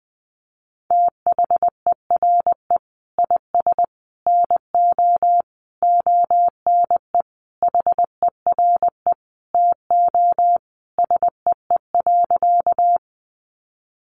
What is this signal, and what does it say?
Morse code, heard as a keyed tone, about 20 wpm, 700 Hz. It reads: THERE IS NO ONE HERE TO SEE.
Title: Sentences from Top 200 Words 20wpm